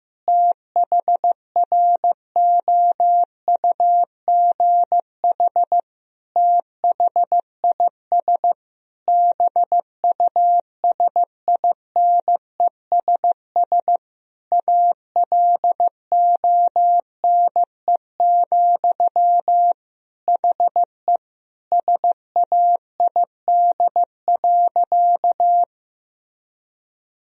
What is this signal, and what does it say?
Morse code, heard as a keyed tone, about 15 wpm, 700 Hz. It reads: THROUGH THIS BUSINESS ALONE, HE SAID.
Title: Sentences from Top 1000 Words - Review Part 2 - 15wpm